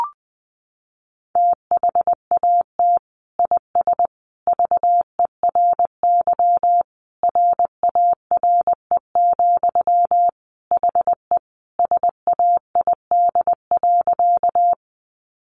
Morse code, heard as a keyed tone, about 20 wpm, 700 Hz. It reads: THAT IS VERY RARE, HE SAID.
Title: Sentences from Top 1100 Words 20wpm